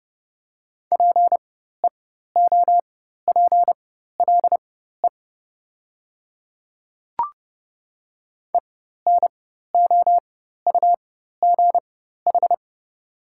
Morse code, heard as a keyed tone, about 30 wpm, spaced as for 14 wpm, 690 Hz. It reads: PEOPLE ENOUGH
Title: 4x Character Spacing - Top 500 Words - 30wpm